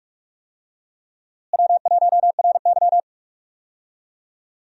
Morse code, heard as a keyed tone, about 46 wpm, 700 Hz. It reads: W1RY
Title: USA Call Signs - 4 Characters - Repeated 3x - 45wpm